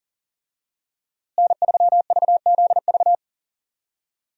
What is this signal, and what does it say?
Morse code, heard as a keyed tone, about 40 wpm, 700 Hz. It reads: N3VZV
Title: USA Call Signs - 5 Characters - Repeated 2x - 40wpm